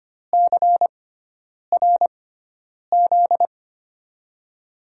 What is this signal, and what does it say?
Morse code, heard as a keyed tone, about 25 wpm, spaced as for 9 wpm, 700 Hz. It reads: CRZ
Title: USA Call Signs - 6x Character Spacing - 25wpm